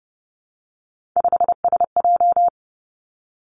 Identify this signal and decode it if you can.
Morse code, heard as a keyed tone, about 30 wpm, 700 Hz. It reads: 5SJ
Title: Three Challenging Letter-Number Combinations 30wpm